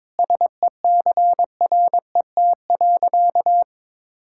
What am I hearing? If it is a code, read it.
Morse code, heard as a keyed tone, about 21 wpm, 700 Hz. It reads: SECRET.
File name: practice / 22wpm / Sentences with 3-Word Phrases - 22wpm